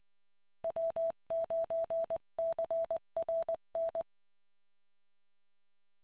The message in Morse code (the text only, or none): W9CRN